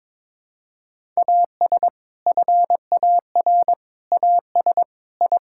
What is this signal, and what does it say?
Morse code, heard as a keyed tone, about 22 wpm, 700 Hz. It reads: AS FAR AS I
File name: practice / 22wpm / Sentences with 4-Word Phrases - 22wpm